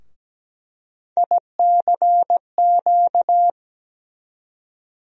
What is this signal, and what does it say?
Morse code, heard as a keyed tone, about 17 wpm, 700 Hz. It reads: ICQ
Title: USA Call Sign Postfixes - 3 Letters 17wpm